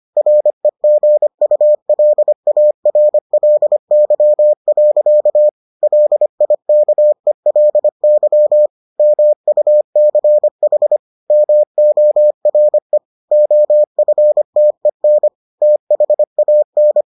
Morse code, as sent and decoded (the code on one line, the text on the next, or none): .-. --. ..- .-.. .- .-. .-.. -.-- .-.-.- / .-.. .. -.- . .-.. -.-- / -- ..- -.-. .... / -- --- .-. / --- ..-. - . -. / - .... .- -.
REGULARLY. LIKELY MUCH MORE OFTEN THAN